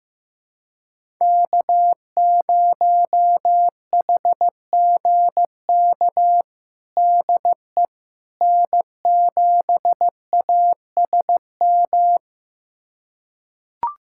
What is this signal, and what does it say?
Morse code, heard as a keyed tone, about 15 wpm, 700 Hz. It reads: K0HGK DE N7ASM
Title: USA Callsign DE Callsign - 5 Characters - 15wpm